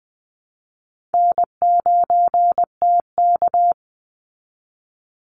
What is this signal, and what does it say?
Morse code, heard as a keyed tone, about 20 wpm, 700 Hz. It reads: N9TK
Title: Easy Call Signs Repeated 3x - 20wpm